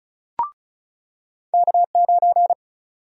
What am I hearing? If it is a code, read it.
Morse code, heard as a keyed tone, about 36 wpm, 700 Hz. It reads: K9
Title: USA Call Sign Prefixes - 35wpm